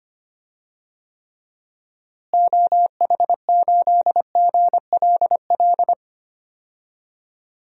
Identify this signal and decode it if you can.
Morse code, heard as a keyed tone, about 25 wpm, 700 Hz. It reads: OH8GLL
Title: DXCC-100 Call Signs - 6 Characters - Repeated 3x - 25wpm